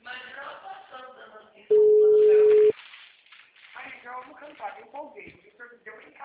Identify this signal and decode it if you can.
Ringback tone (Europe, 425 Hz), with speech.
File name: out- -2001- - - .37305